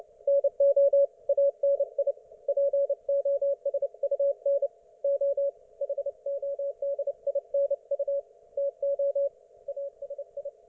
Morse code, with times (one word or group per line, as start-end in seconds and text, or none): 0.3-1.1 NO
1.3-2.1 ANI
2.5-4.7 POSUN
5.0-5.5 O
5.8-8.2 HODINU
8.6-9.3 TO
9.7-10.5 ASI